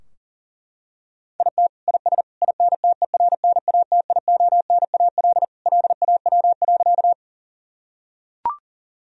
Morse code, heard as a keyed tone, about 40 wpm, 700 Hz. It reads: IT IS INTERNATIONAL LAW.
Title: Sentences from Top 1000 Words - Review Part 4 - 40wpm